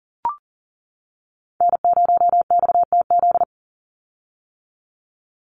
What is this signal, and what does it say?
Morse code, heard as a keyed tone, about 40 wpm, 700 Hz. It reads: N0XTZ
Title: USA Call Signs 40wpm